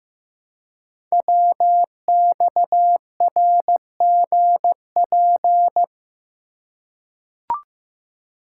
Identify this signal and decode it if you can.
Morse code, heard as a keyed tone, about 15 wpm, 700 Hz. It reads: WXRGP
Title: Code Groups - 5 Letters - 15wpm